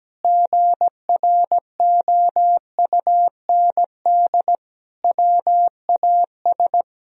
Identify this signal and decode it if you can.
Morse code, heard as a keyed tone, about 17 wpm, 700 Hz. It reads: GROUND WAS
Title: Sentences from Top 1000 Words - Review Part 4 - 17wpm